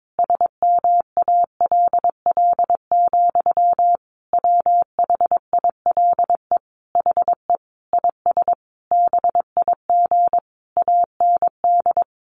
Morse code, heard as a keyed tone, about 22 wpm, 700 Hz. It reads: SMALL, WHILE HE IS BIG AND
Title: Sentences from Top 1300 Words 22wpm